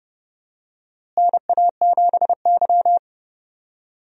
Morse code, heard as a keyed tone, about 32 wpm, 700 Hz. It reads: NA7Y
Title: USA Call Signs - 4 Characters - Repeated 3x - 30wpm